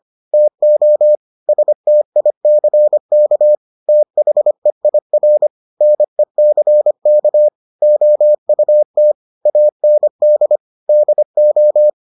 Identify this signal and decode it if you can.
Morse code, heard as a keyed tone, about 25 wpm, 600 Hz. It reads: TO STICK THEIR NECK OUT AND DO